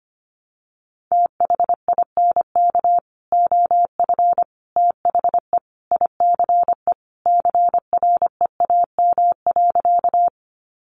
Morse code, heard as a keyed tone, about 25 wpm, 700 Hz. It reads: THINK OF THE ICE CREAM.
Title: Sentences from Top 1500 Words 25wpm